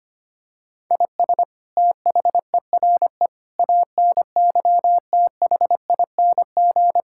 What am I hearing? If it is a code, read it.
Morse code, heard as a keyed tone, about 25 wpm, 700 Hz. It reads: IS THERE ANYTHING
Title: Sentences from Top 1000 Words - Review Part 6 - 25wpm